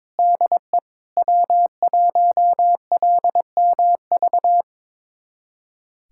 Morse code, heard as a keyed tone, about 22 wpm, 700 Hz. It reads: DE W1LMV
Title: USA Callsign DE Callsign - 5 Characters - 22wpm